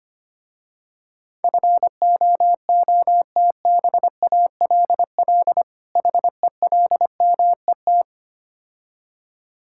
Morse code, heard as a keyed tone, about 25 wpm, 700 Hz. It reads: FOOTBALL HELMET
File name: practice / 25wpm / Difficult Words with Hints - 25wpm